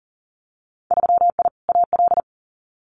Morse code, heard as a keyed tone, about 40 wpm, 700 Hz. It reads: 3I AL
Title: Winter Field Day Exchanges Encore 40wpm